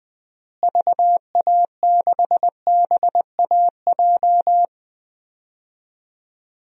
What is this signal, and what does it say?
Morse code, heard as a keyed tone, about 20 wpm, 700 Hz. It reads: VA6BAJ
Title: DXCC-100 Call Signs - 6 Characters - Repeated 3x - 20wpm